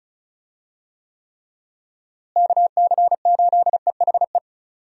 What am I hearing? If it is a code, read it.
Morse code, heard as a keyed tone, about 34 wpm, 700 Hz. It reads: KC8EHE
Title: USA Call Signs - 6 Characters - Repeated 3x - 35wpm